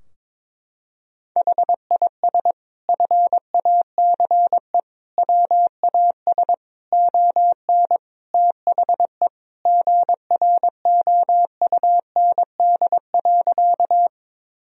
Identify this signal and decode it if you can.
Morse code, heard as a keyed tone, about 22 wpm, 700 Hz. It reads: HIS FACE WAS ON THE GROUND.
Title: Sentences from Top 1000 Words - Review Part 2 - 22wpm